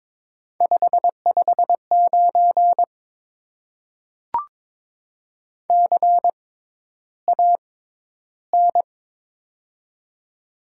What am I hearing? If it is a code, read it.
Morse code, heard as a keyed tone, about 22 wpm, 700 Hz. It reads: 5 5 9 C A N
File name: Top 90 QSO Elements - 6x Character Spacing - 22wpm